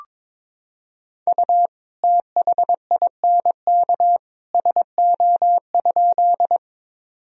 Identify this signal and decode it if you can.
Morse code, heard as a keyed tone, about 22 wpm, 700 Hz. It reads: U THINK SO?
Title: Sentences from Top 200 Words and Abbreviations 22wpm